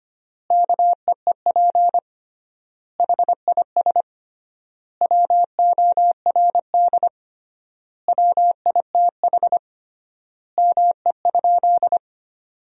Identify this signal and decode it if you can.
Morse code, heard as a keyed tone, about 25 wpm, 700 Hz. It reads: KEEP HIS WORD WITH ME?
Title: 3x Word-Spacing - Sentences from Top 300 Words - 25wpm